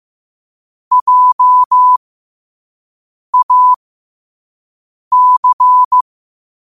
Morse code, heard as a keyed tone, about 15 wpm, spaced as for 5 wpm, 1000 Hz. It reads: JAC
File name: qotdfiles / 5wpm